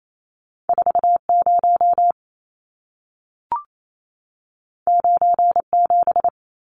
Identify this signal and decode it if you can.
Morse code, heard as a keyed tone, about 28 wpm, 700 Hz. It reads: 40 97
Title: Numbers - 2-Digit 28wpm